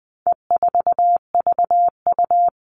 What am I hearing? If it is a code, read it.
Morse code, heard as a keyed tone, about 20 wpm, 700 Hz. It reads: E4VU
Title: All Cut Numbers - 4-Digit 20wpm